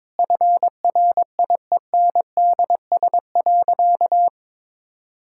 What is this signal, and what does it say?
Morse code, heard as a keyed tone, about 22 wpm, 700 Hz. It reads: FRIENDS.
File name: practice / 22wpm / Sentences with 4-Word Phrases - 22wpm